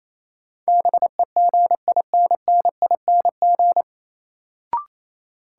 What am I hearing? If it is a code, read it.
Morse code, heard as a keyed tone, about 28 wpm, 700 Hz. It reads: BEGINNING
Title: Sentences from Top 800 Words - New Word Drill - 28wpm